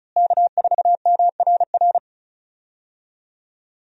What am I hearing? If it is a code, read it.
Morse code, heard as a keyed tone, about 35 wpm, 700 Hz. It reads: K4MRR
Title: USA Call Signs - 5 Characters - Repeated 3x - 35wpm